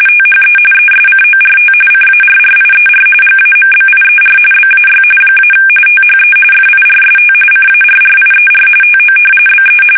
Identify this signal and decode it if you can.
RTTY, about 45.45 baud. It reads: THE ESCORT CARRIER GAMBIER BAY - THE FIRST CARRIER TO BE
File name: fsk850 / battleshipnj